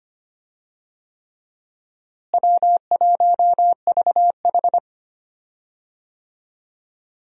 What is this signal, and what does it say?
Morse code, heard as a keyed tone, about 25 wpm, 700 Hz. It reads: W1VH
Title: USA Call Signs - 4 Characters - Repeated 3x - 25wpm